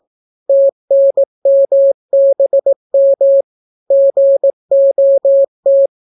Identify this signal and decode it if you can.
Morse code, heard as a keyed tone, about 18 wpm, 550 Hz. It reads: TNMBM GOT